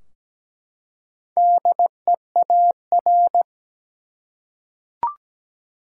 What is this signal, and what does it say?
Morse code, heard as a keyed tone, about 17 wpm, 700 Hz. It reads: DEAR
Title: Four Letter Words - 17wpm